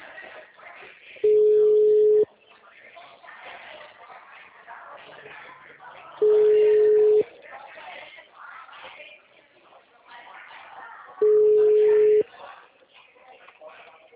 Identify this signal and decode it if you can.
Ringback tone (Europe, 425 Hz), with speech.
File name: out- -2010